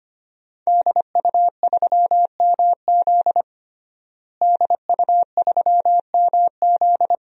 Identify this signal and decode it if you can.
Morse code, heard as a keyed tone, about 25 wpm, 700 Hz. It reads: DU3MZ DU3MZ
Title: DXCC-100 Call Signs - 5 Characters - Repeated 2x - 25wpm